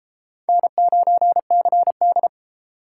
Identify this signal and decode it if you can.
Morse code, heard as a keyed tone, about 30 wpm, 700 Hz. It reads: N9CD